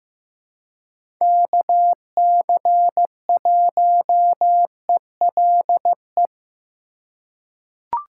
Morse code, heard as a keyed tone, about 15 wpm, 700 Hz. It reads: KC1ELE
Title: USA Call Signs - 6 Characters - 2x Character Spacing - 15wpm